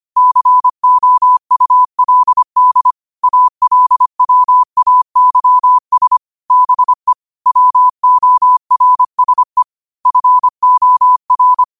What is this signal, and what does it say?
Morse code, heard as a keyed tone, about 25 wpm, 1000 Hz. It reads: COULD ALWAYS BE WORSE FOR
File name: qotdfiles / 25wpm